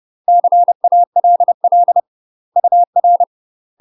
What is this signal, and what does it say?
Morse code, heard as a keyed tone, about 30 wpm, 700 Hz. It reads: CALL UR
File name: UK MASTER QSOs TRAINING / 30WPM / UK MASTER QSOs 0034